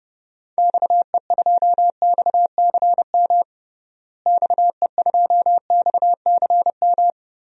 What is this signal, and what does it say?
Morse code, heard as a keyed tone, about 32 wpm, 700 Hz. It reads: XE2XCM XE2XCM
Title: DXCC-100 Call Signs - 6 Characters - Repeated 3x - 30wpm